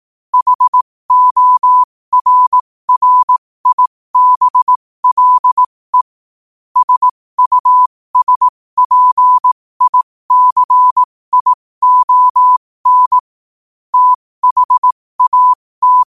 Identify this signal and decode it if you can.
Morse code, heard as a keyed tone, about 18 wpm, spaced as for 15 wpm, 1000 Hz. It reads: HORRIBLE SUSPICION THAT